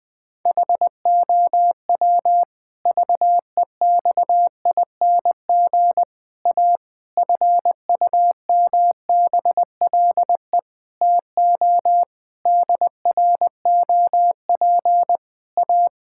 Morse code, heard as a keyed tone, about 20 wpm, 700 Hz. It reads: HOW VEXING A FUMBLE TO DROP A
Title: Pangram Challenge v2 - 20wpm